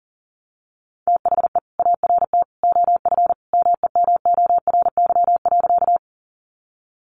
Morse code, heard as a keyed tone, about 40 wpm, 700 Hz. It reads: THE ART OF MEMORY.
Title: Sentences from Top 1300 Words 40wpm